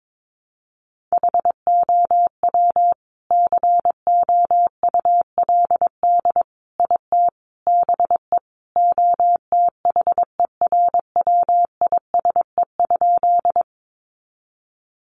Morse code, heard as a keyed tone, about 22 wpm, 700 Hz. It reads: HOW COULD IT BE OTHERWISE?
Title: Sentences from Top 1200 Words 22wpm